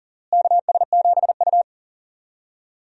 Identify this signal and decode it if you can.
Morse code, heard as a keyed tone, about 40 wpm, 700 Hz. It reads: KS7U
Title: USA Call Signs - 4 Characters - Repeated 3x - 40wpm